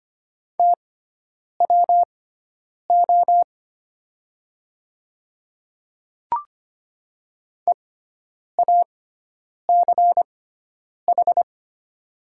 Morse code, heard as a keyed tone, about 25 wpm, spaced as for 9 wpm, 700 Hz. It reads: TWO EACH
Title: 6x Character Spacing - Top 200 Words - 25wpm